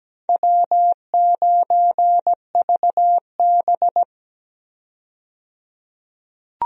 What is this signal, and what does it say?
Morse code, heard as a keyed tone, about 17 wpm, 700 Hz. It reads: W9VB
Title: USA Call Signs - 4 Characters - 17wpm